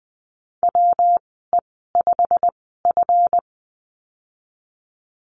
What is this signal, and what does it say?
Morse code, heard as a keyed tone, about 20 wpm, spaced as for 15 wpm, 700 Hz. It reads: WE5F
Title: 161 Lesson 26g - Letter g - Call Signs 20wpm